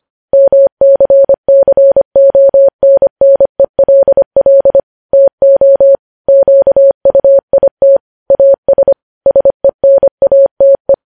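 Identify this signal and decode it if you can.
Morse code, heard as a keyed tone, about 25 wpm, 550 Hz. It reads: MCCONNELL TO QUIT AS SENATE
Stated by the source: News headlines sent in Morse at 25 wpm.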